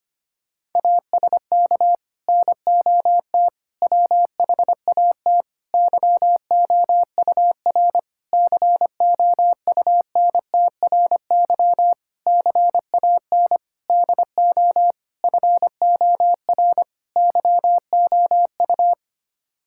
Morse code, heard as a keyed tone, about 25 wpm, 700 Hz. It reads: ASK NOT WHAT YOUR COUNTRY CAN DO FOR YOU
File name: Sets of 9 Words - 25wpm